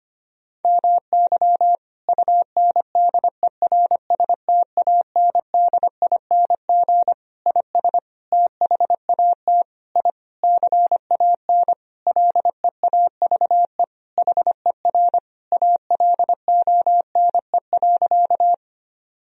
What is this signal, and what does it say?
Morse code, heard as a keyed tone, about 25 wpm, 700 Hz. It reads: MY UNDERSTANDING IS THAT I CAN LEAVE HER ALONE.
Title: Sentences from Top 1000 Words - Review Part 3 - 25wpm